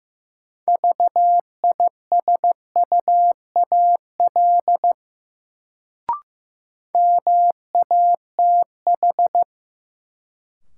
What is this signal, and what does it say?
Morse code, heard as a keyed tone, about 16 wpm, 700 Hz. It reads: VISUAL MATH
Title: Sentences from Top 1000 Words - Review Part 6 - New Word Drill - 15wpm